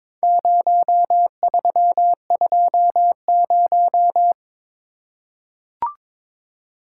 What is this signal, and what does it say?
Morse code, heard as a keyed tone, about 22 wpm, 700 Hz. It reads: 0320
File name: practice / 22wpm / Four Letter-Number Combinations - 22wpm